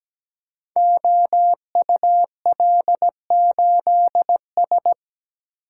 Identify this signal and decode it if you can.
Morse code, heard as a keyed tone, about 17 wpm, 700 Hz. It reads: OUL8S
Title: Code Groups - 5 Letter:Numbers - 17wpm